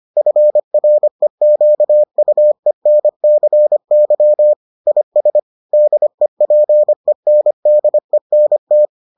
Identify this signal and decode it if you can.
Morse code, heard as a keyed tone, about 25 wpm, 600 Hz. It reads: FREQUENCY IS DEPENDENT